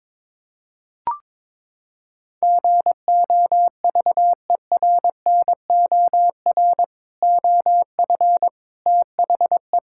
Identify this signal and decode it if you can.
Morse code, heard as a keyed tone, about 22 wpm, 700 Hz. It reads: GOVERNOR OF THE
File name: practice / 22wpm / Sentences with 3-Word Phrases - 22wpm